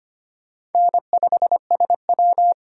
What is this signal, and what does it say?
Morse code, heard as a keyed tone, about 33 wpm, 700 Hz. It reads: N5SW